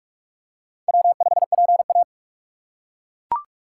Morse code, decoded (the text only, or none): W5PA